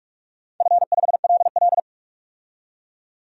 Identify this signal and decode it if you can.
Morse code, heard as a keyed tone, about 47 wpm, 700 Hz. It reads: F5LL
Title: DXCC-100 Call Signs - 4 Characters - 45wpm